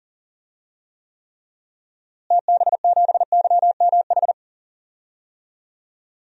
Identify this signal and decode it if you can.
Morse code, heard as a keyed tone, about 40 wpm, 700 Hz. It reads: TB7YMH
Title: DXCC-100 Call Signs - 6 Characters - Repeated 3x - 40wpm